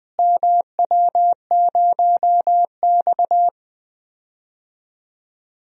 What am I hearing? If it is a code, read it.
Morse code, heard as a keyed tone, about 20 wpm, 700 Hz. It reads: MW0X
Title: DXCC-100 Call Signs - 4 Characters - Repeated 2x - 20wpm